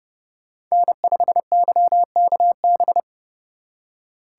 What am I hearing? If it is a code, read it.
Morse code, heard as a keyed tone, about 36 wpm, 700 Hz. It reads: N5YKB
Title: Speed Racing - USA Call Signs 30wpm